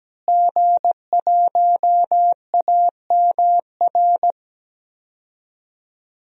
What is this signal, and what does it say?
Morse code, heard as a keyed tone, about 17 wpm, 700 Hz. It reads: G1AMR
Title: DXCC-100 Call Signs - 5 Characters - Repeated 3x - 17wpm